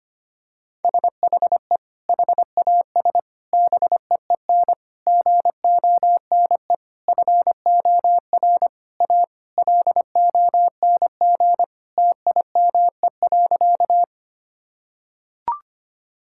Morse code, heard as a keyed tone, about 25 wpm, 700 Hz. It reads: SHE HAS BEEN GONE FOR A LONG TIME.
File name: practice / 25wpm / Sentences with 3-Word Phrases - 25wpm